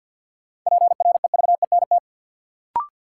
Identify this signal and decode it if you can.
Morse code, heard as a keyed tone, about 52 wpm, 700 Hz. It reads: PREVENT
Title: Sentences from Top 1100 Words - New Word Drill - 50wpm